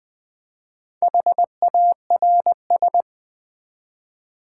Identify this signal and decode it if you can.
Morse code, heard as a keyed tone, about 20 wpm, 700 Hz. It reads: HARS
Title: Build A Word - 1 Letter At A Time - 5-Letter Words - 20wpm